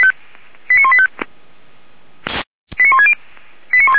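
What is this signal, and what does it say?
Five-tone selective call (ZVEI), heard as quick runs of five tones, tones of about 70 ms. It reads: 89185 89160
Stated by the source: Received on an SDR.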